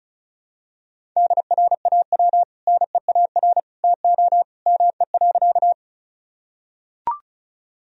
Morse code, heard as a keyed tone, about 36 wpm, 700 Hz. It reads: DRAW NEAR TO ME.
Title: Sentences from Top 1000 Words - Review Part 1 - 35wpm